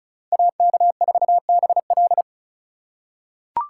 Morse code, heard as a keyed tone, about 36 wpm, 700 Hz. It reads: AK4BL
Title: USA Call Signs - 2x Character Spacing - 35wpm